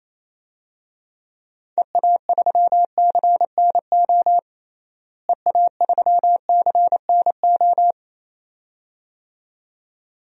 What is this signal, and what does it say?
Morse code, heard as a keyed tone, about 28 wpm, 700 Hz. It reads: EA3CNO EA3CNO
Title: DXCC-100 Call Signs - 6 Characters - Repeated 2x - 28wpm